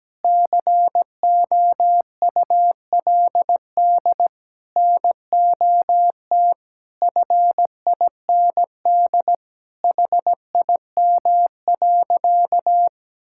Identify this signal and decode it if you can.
Morse code, heard as a keyed tone, about 17 wpm, 700 Hz. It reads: COULD NOT FIND HIM.
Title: Sentences from Top 1000 Words - Review Part 5 - 17wpm